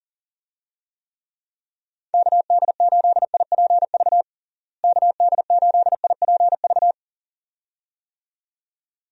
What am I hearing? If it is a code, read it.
Morse code, heard as a keyed tone, about 40 wpm, 700 Hz. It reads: KD8IPV KD8IPV
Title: USA Call Signs - 6 Characters - Repeated 2x - 40wpm